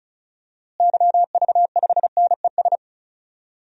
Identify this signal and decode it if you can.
Morse code, heard as a keyed tone, about 36 wpm, 700 Hz. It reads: YV5NES
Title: DXCC-100 Call Signs - 6 Characters - Repeated 2x - 35wpm